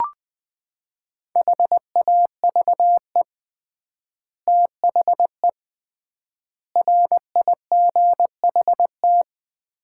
Morse code, heard as a keyed tone, about 20 wpm, 700 Hz. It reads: HAVE THE RIGHT
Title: 3x Word-Spacing - Sets of 4 Words - 20wpm